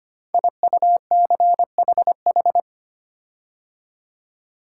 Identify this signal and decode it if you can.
Morse code, heard as a keyed tone, about 25 wpm, 700 Hz. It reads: IUCHH
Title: Code Groups - 5 Letters - 25wpm